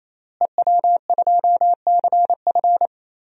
Morse code, heard as a keyed tone, about 28 wpm, 700 Hz. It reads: EW2CF
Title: DXCC-100 Call Signs - 5 Characters - Repeated 3x - 28wpm